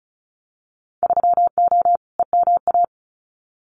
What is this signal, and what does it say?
Morse code, heard as a keyed tone, about 36 wpm, 690 Hz. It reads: 3O EMA
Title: Winter Field Day Exchanges 35wpm